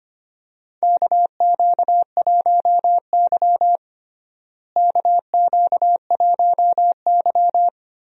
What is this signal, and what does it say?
Morse code, heard as a keyed tone, about 25 wpm, 700 Hz. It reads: KQ1Y KQ1Y
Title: USA Call Signs - 4 Characters - Repeated 2x - 25wpm